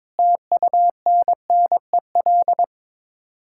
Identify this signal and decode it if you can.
Morse code, heard as a keyed tone, about 22 wpm, 700 Hz. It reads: TUNNEL
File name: practice / 22wpm / Sets of 8 Words - 22wpm